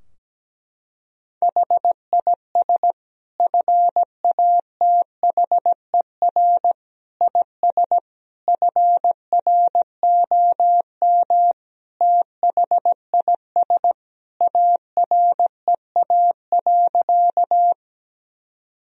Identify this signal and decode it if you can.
Morse code, heard as a keyed tone, about 17 wpm, 700 Hz. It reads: HIS FATHER IS FROM THIS AREA.
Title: Sentences from Top 1000 Words - Review Part 2 - 17wpm